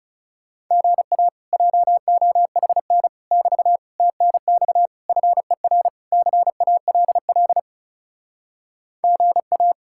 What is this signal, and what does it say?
Morse code, heard as a keyed tone, about 39 wpm, 700 Hz. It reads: GA JOHN <BT> TNX FER CALL GA